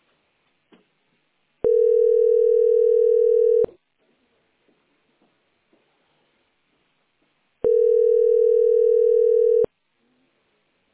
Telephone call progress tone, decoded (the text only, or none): ringback tone